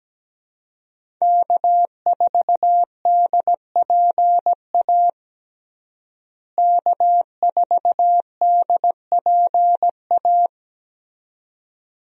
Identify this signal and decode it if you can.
Morse code, heard as a keyed tone, about 17 wpm, 700 Hz. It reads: K4DPA K4DPA
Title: USA Call Signs - 5 Characters - Repeated 2x - 17wpm